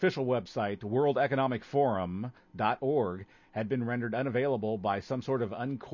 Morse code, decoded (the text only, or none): none